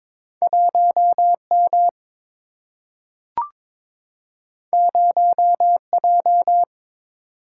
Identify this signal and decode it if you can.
Morse code, heard as a keyed tone, about 22 wpm, 700 Hz. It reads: 1M 0J
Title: Single-Letter Single-Number 22wpm